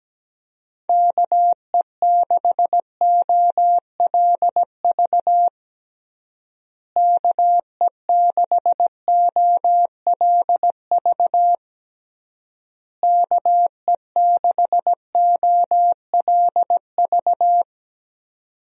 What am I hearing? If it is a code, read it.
Morse code, heard as a keyed tone, about 17 wpm, 700 Hz. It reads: KE6OLV KE6OLV KE6OLV
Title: USA Call Signs - 6 Characters - Repeated 3x - 17wpm